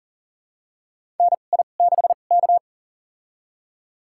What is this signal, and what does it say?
Morse code, heard as a keyed tone, about 40 wpm, spaced as for 29 wpm, 700 Hz. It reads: NI6K
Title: USA Call Signs - 2x Character Spacing - 40wpm